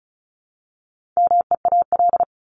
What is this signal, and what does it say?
Morse code, heard as a keyed tone, about 34 wpm, 700 Hz. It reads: MEAL
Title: Top 1751-2000 Words 35wpm